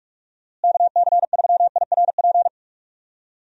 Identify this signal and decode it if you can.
Morse code, heard as a keyed tone, about 46 wpm, 700 Hz. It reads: KC3IRP
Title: USA Call Signs - 6 Characters - Repeated 3x - 45wpm